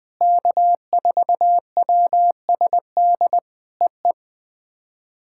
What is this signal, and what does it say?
Morse code, heard as a keyed tone, about 20 wpm, 700 Hz. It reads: K4WSD EE
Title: POTA QSO Practice - 20wpm